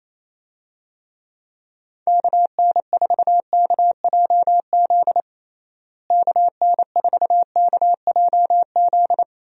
Morse code, heard as a keyed tone, about 28 wpm, 700 Hz. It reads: KN4KJZ KN4KJZ
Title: USA Call Signs - 6 Characters - Repeated 2x - 28wpm